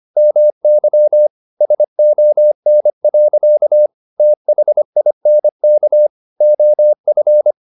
Morse code, heard as a keyed tone, about 25 wpm, 600 Hz. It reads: MY SON. THINK OF
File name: the old man and the sea part 1 25wpm